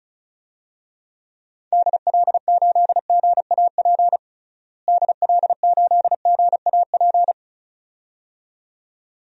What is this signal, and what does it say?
Morse code, heard as a keyed tone, about 36 wpm, 700 Hz. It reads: DL8GAP DL8GAP
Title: DXCC-100 Call Signs - 6 Characters - Repeated 2x - 35wpm